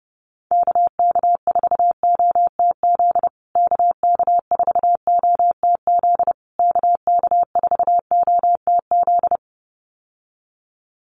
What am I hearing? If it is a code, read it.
Morse code, heard as a keyed tone, about 30 wpm, 700 Hz. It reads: KK4OTZ KK4OTZ KK4OTZ
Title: USA Call Signs Repeated 3x 30wpm